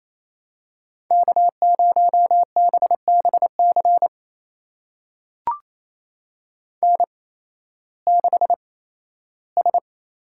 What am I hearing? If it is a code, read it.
Morse code, heard as a keyed tone, about 28 wpm, spaced as for 15 wpm, 700 Hz. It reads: K0BBC N6S
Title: USA Call Signs - 5 Characters - 8x Character Spacing - 28wpm